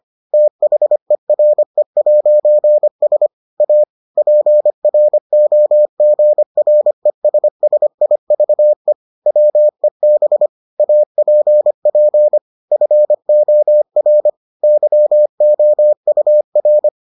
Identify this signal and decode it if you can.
Morse code, heard as a keyed tone, about 25 wpm, 600 Hz. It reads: THERE'S A PROGRESSIVE WEB APP FOR YOUR